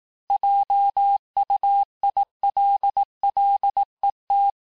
Morse code, heard as a keyed tone, about 18 wpm, 800 Hz. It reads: JUILLET
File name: cours-du-lundi files / cours-1852-texte-clair-vitesse-17MPM-puis-22MPM